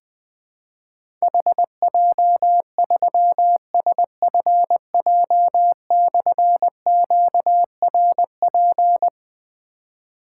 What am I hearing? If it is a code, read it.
Morse code, heard as a keyed tone, about 20 wpm, 700 Hz. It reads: HJ3SFJ/QRP
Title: Difficult Call Signs 20wpm